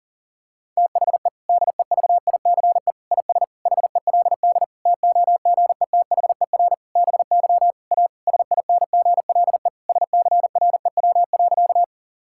Morse code, heard as a keyed tone, about 40 wpm, 700 Hz. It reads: THE DEVICE IS HELD TOGETHER BY A SINGLE SCREW.
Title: Sentences from Top 1600 Words 40wpm